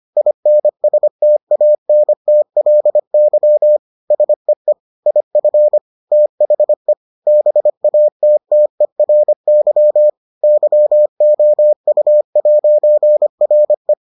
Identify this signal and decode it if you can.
Morse code, heard as a keyed tone, about 25 wpm, 600 Hz. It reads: INSTANTLY SEE IF THE BATTERY YOU'RE